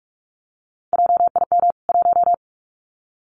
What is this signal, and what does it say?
Morse code, heard as a keyed tone, about 44 wpm, 700 Hz. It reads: JIM 1
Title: CWT Exchange 45wpm